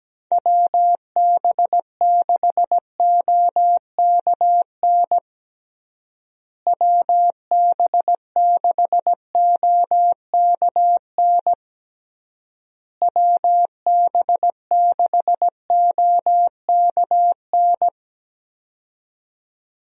Morse code, heard as a keyed tone, about 17 wpm, 700 Hz. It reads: WB6OKN WB6OKN WB6OKN
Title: USA Call Signs - 6 Characters - Repeated 3x - 17wpm